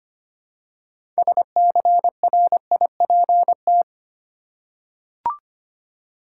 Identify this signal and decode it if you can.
Morse code, heard as a keyed tone, about 25 wpm, 700 Hz. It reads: SCRIPT
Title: Six Letter Words - 25wpm